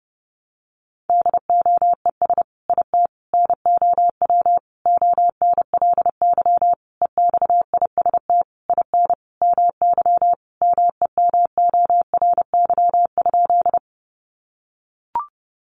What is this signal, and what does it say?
Morse code, heard as a keyed tone, about 30 wpm, 700 Hz. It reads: DOES IT NOW ONLY EXIST IN MY MEMORY?
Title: Sentences from Top 1300 Words 30wpm